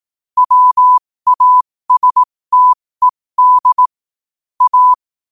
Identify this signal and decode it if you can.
Morse code, heard as a keyed tone, about 18 wpm, spaced as for 15 wpm, 1000 Hz. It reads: WASTED A
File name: qotdfiles / 15wpm